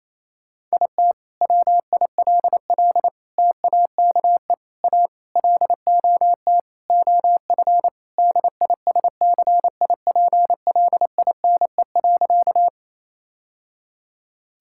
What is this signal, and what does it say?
Morse code, heard as a keyed tone, about 28 wpm, 700 Hz. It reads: IT WILL TAKE A LOT OF DISCIPLINE.
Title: Sentences from Top 1000 Words - Review Part 6 - 28wpm